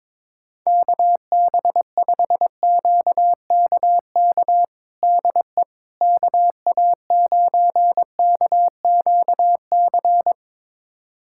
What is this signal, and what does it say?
Morse code, heard as a keyed tone, about 22 wpm, 700 Hz. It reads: KB5QKK DE KA9KQC
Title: USA Callsign DE Callsign - 6 Characters - 22wpm